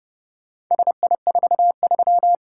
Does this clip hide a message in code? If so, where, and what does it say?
Yes: Morse code from 0.7 to 2.4 s, heard as a keyed tone, about 29 wpm, 700 Hz. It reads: SI43